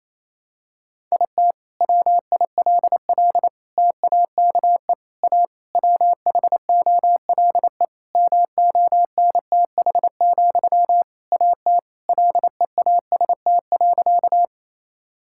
Morse code, heard as a keyed tone, about 27 wpm, 700 Hz. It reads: IT WILL TAKE A WHOLE MONTH, AT LEAST.
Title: Sentences from Top 1000 Words - Review Part 5 - 28wpm